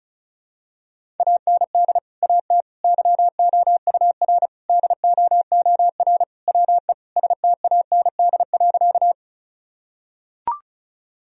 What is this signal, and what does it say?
Morse code, heard as a keyed tone, about 36 wpm, 700 Hz. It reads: AND AT YOUR DOOR WE STAND.
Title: Sentences from Top 1000 Words - Review Part 1 - 35wpm